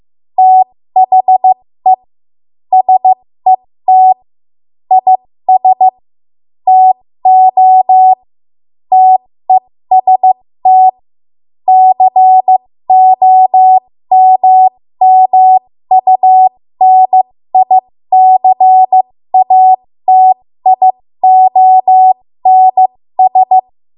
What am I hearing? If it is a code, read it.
Morse code, heard as a keyed tone, about 15 wpm, spaced as for 13 wpm, 750 Hz. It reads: THE SET IS TO TEST COMMUNICATIONS
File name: Morse / 13WPM